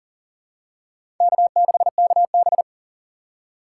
Morse code, heard as a keyed tone, about 42 wpm, 690 Hz. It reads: K6KB
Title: USA Call Signs - 4 Characters - 2x Character Spacing - 40wpm